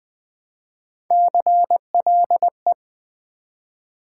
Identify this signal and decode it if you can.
Morse code, heard as a keyed tone, about 20 wpm, 700 Hz. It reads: CLE
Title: Build A Word - 1 Letter At A Time - 5-Letter Words - 20wpm